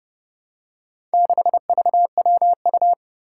Morse code, heard as a keyed tone, about 29 wpm, 700 Hz. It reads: 6VWU
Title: Four Challenging Letter-Number Combinations 30wpm